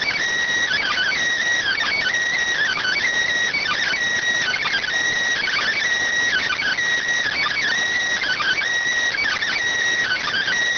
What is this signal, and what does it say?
Emergency Alert System (SAME) tones (header bursts), heard as harsh buzzing bursts. It is heard from 0.7 to 10.5 s.